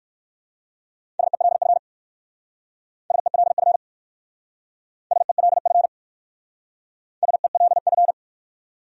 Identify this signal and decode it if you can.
Morse code, heard as a keyed tone, about 56 wpm, 700 Hz. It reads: SELF SELF SELF SELF
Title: Speed-Racing Top 200 Words 45wpm